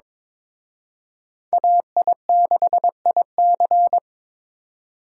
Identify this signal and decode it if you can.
Morse code, heard as a keyed tone, about 26 wpm, 700 Hz. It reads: AI6IC
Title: Speed Racing - USA 2x2 Call Signs 22wpm